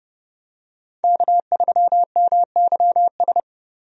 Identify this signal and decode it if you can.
Morse code, heard as a keyed tone, about 30 wpm, 700 Hz. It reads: K3MYS
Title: USA Call Signs - 5 Characters - 30wpm